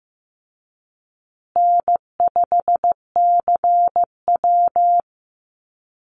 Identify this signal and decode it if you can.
Morse code, heard as a keyed tone, about 15 wpm, 700 Hz. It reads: N5CW
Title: Easy Call Signs Repeated 3x - 15wpm